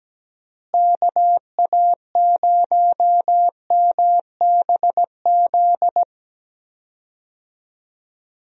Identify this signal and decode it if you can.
Morse code, heard as a keyed tone, about 17 wpm, 700 Hz. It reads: KA0MBZ
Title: USA Call Signs - 6 Characters - Repeated 2x - 17wpm